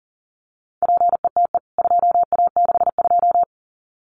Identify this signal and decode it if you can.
Morse code, heard as a keyed tone, about 40 wpm, 700 Hz. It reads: PETE 2A62
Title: CWT Exchange 40wpm